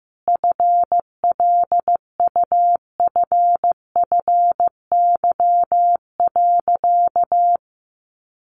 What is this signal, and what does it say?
Morse code, heard as a keyed tone, about 15 wpm, 700 Hz. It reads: FLUFFY.
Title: Easy Phrases 15wpm